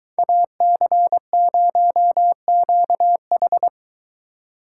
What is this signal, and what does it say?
Morse code, heard as a keyed tone, about 23 wpm, 700 Hz. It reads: AC0QH